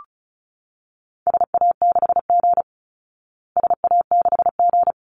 Morse code, heard as a keyed tone, about 34 wpm, 700 Hz. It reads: SA6G SA6G
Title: Call Signs Repeated 3x - 35wpm